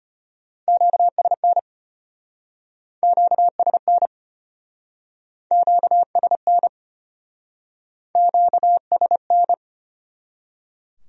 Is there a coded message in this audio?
Yes: Morse code from 0.7 to 9.5 s, heard as a keyed tone, about 34 wpm, 700 Hz. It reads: QSN QSN QSN QSN